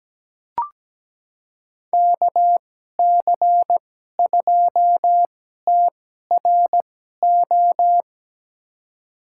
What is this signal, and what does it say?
Morse code, heard as a keyed tone, about 17 wpm, spaced as for 12 wpm, 700 Hz. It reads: KC2TRO
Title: USA Call Signs - 6 Characters - 2x Character Spacing - 17wpm